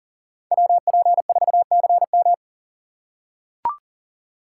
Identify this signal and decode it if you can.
Morse code, heard as a keyed tone, about 40 wpm, 700 Hz. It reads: WP4CM
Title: USA Call Signs - 5 Characters - 3x Character Spacing - 40wpm